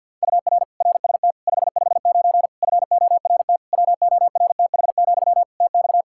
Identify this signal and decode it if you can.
Morse code, heard as a keyed tone, about 52 wpm, 700 Hz. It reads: UR RST 559 FORT WORTH, TX
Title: Practice QSO - Exchange 1 - Beginning - 50wpm